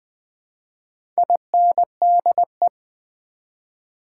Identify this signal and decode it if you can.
Morse code, heard as a keyed tone, about 20 wpm, 700 Hz. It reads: INDE
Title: Build A Word - 1 Letter At A Time - 3 to 5-Letter Words - 20wpm